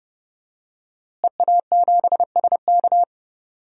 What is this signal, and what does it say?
Morse code, heard as a keyed tone, about 30 wpm, 700 Hz. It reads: EA7SK
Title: DXCC-100 Call Signs - 5 Characters - Repeated 3x - 30wpm